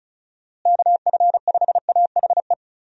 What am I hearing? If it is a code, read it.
Morse code, heard as a keyed tone, about 34 wpm, 700 Hz. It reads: KF5AHE
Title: USA Call Signs - 6 Characters - Repeated 2x - 35wpm